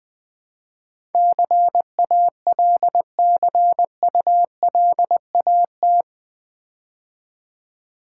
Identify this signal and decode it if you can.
Morse code, heard as a keyed tone, about 20 wpm, 700 Hz. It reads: CALCULAT
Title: Build A Word - 4 Letters At A Time - 11-Letter Words - 20wpm